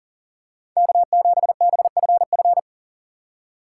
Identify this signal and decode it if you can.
Morse code, heard as a keyed tone, about 40 wpm, 700 Hz. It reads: K7BFF
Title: USA Call Signs - 5 Characters - 6x Character Spacing - 40wpm